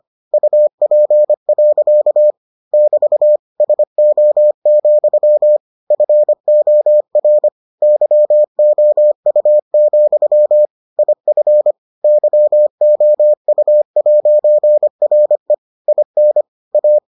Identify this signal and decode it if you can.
Morse code, heard as a keyed tone, about 25 wpm, 600 Hz. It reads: UP. <BT> SO, FOR YOU, IF YOU'RE IN A